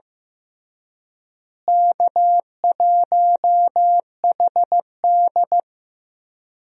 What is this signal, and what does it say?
Morse code, heard as a keyed tone, about 15 wpm, 700 Hz. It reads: K1HD
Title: USA Call Signs - 4 Characters - Repeated 3x - 15wpm